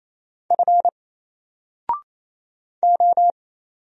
Morse code, heard as a keyed tone, about 27 wpm, 700 Hz. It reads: F O
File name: Single-Letter-Number-28wpm